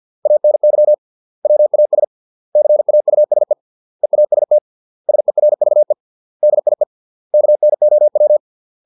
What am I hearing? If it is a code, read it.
Morse code, heard as a keyed tone, about 50 wpm, 600 Hz. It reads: ANY WAS CAUSE EAST SERVE DIE KNOW